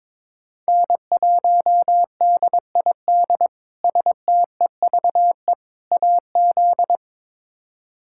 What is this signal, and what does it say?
Morse code, heard as a keyed tone, about 22 wpm, 700 Hz. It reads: N1DID STEVE AZ
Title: K1USN SST QSO Practice - 22wpm